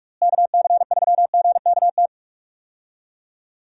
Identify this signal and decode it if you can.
Morse code, heard as a keyed tone, about 42 wpm, 700 Hz. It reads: KC3GKT
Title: USA Call Signs - 6 Characters - Repeated 3x - 45wpm